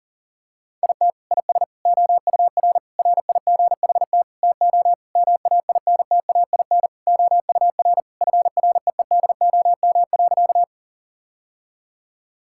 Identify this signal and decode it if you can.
Morse code, heard as a keyed tone, about 40 wpm, 700 Hz. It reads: IT IS OUR RIGHT TO MAINTAIN OUR FREEDOM.
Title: Sentences from Top 1000 Words - Review Part 5 - 40wpm